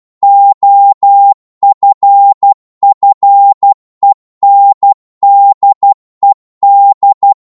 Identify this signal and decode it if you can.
Morse code, heard as a keyed tone, about 12 wpm, 800 Hz. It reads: OFFENDED
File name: Morse / psg-12wpm